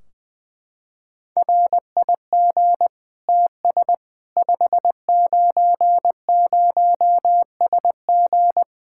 Morse code, heard as a keyed tone, about 20 wpm, 700 Hz. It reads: RIG TS 590SG
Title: Practice QSO - Exchange 2 - Info - 20wpm